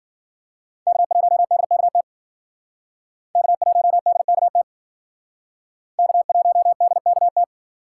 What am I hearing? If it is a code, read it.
Morse code, heard as a keyed tone, about 48 wpm, 700 Hz. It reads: K1DKT K1DKT K1DKT
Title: Speed Racing - USA Call Signs 40wpm